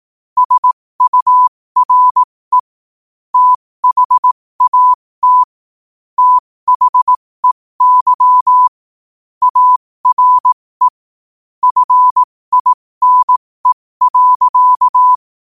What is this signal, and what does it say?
Morse code, heard as a keyed tone, about 18 wpm, spaced as for 15 wpm, 1000 Hz. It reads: SURE THAT THEY ARE FINE.
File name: qotdfiles / 15wpm